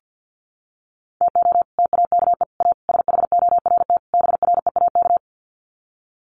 Morse code, heard as a keyed tone, about 48 wpm, 700 Hz. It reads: TO TAKE A SHORT BREAK
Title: Sets of 5 Words Encore2 50wpm